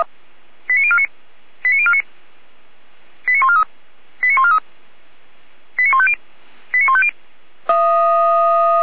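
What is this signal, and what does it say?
Five-tone selective call (ZVEI), heard as quick runs of five tones, tones of about 70 ms. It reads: 89049 89049 89143 89143 89150 89150
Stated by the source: Five-tone ZVEI selective call received by SDR, Feuerwehr Orlamünde.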